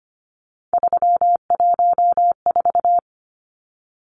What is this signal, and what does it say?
Morse code, heard as a keyed tone, about 25 wpm, 700 Hz. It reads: 314
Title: Numbers - 3-Digit 25wpm